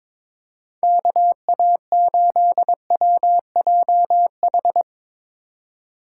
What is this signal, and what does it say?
Morse code, heard as a keyed tone, about 22 wpm, 700 Hz. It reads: KA8WJH
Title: USA Call Signs - 4x Character Spacing - 22wpm